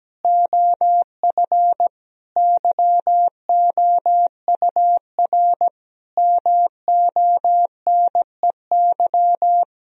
Morse code, heard as a keyed tone, about 17 wpm, 700 Hz. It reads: OF YOUR MONEY
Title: Sentences from Top 1000 Words - Review Part 4 - 17wpm